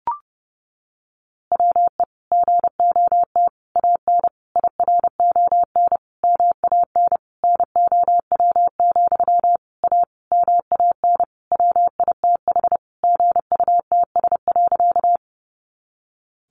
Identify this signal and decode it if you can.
Morse code, heard as a keyed tone, about 30 wpm, 700 Hz. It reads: WE GOT AN IRON MAN NOW, A MAN WITH GUTS.
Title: A Sound Of Thunder 30wpm